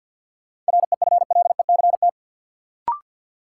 Morse code, heard as a keyed tone, about 53 wpm, 700 Hz. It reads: REFLECT
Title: Sentences from Top 1500 Words - New Word Drill Alt - 50wpm